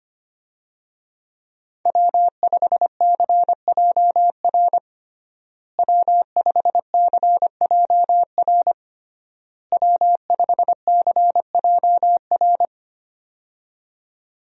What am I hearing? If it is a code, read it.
Morse code, heard as a keyed tone, about 25 wpm, 700 Hz. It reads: W5CJR W5CJR W5CJR
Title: USA Call Signs - 5 Characters - Repeated 3x - 25wpm